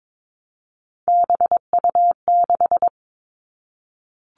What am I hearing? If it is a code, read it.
Morse code, heard as a keyed tone, about 22 wpm, 700 Hz. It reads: BU6
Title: Three Challenging Letter-Number Combinations 22wpm